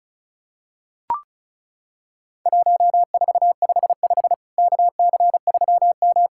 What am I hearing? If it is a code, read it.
Morse code, heard as a keyed tone, about 36 wpm, 700 Hz. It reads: 1455 KC3M
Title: Worked All Europe DX Contest - QTC Exchange - 35wpm